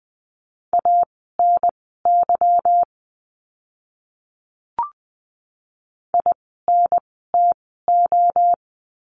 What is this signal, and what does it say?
Morse code, heard as a keyed tone, about 20 wpm, spaced as for 15 wpm, 700 Hz. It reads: ANY INTO
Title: 237 Lesson 38f - Letter q - Words 20wpm